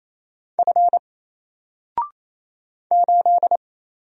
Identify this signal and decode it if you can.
Morse code, heard as a keyed tone, about 27 wpm, 700 Hz. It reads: F 8
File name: Single-Letter-Number-28wpm